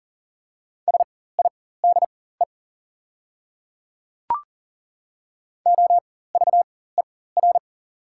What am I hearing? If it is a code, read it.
Morse code, heard as a keyed tone, about 40 wpm, spaced as for 19 wpm, 700 Hz. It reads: SIDE OVER
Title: 4x Character Spacing - Top 200 Words - 40wpm